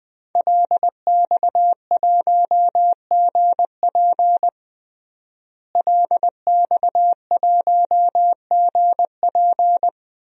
Morse code, heard as a keyed tone, about 19 wpm, 700 Hz. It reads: LX1GP LX1GP
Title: DXCC-100 Call Signs - 5 Characters - Repeated 2x - 20wpm